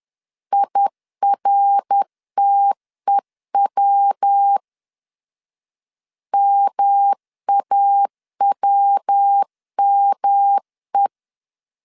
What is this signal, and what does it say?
Morse code, heard as a keyed tone, about 10 wpm, 800 Hz. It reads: IRTEW MAWME
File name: CW / DET / 1 3 1500C S B 10wpm 3335